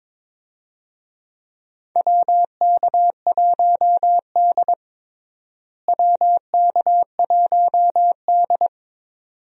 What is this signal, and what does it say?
Morse code, heard as a keyed tone, about 21 wpm, 700 Hz. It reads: WK1D WK1D
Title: USA Call Signs - 4 Characters - Repeated 2x - 22wpm